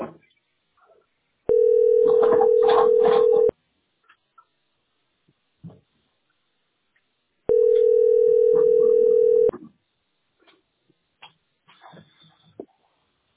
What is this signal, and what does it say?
Ringback tone (North America), 440 + 480 Hz.